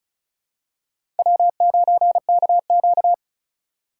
Morse code, heard as a keyed tone, about 36 wpm, 700 Hz. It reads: W9KQ
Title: USA Call Signs - 2x Character Spacing - 35wpm